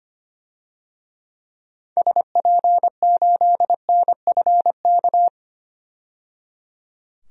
Morse code, heard as a keyed tone, about 25 wpm, 700 Hz. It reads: SP8NFK
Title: DXCC-100 Call Signs - 6 Characters - 25wpm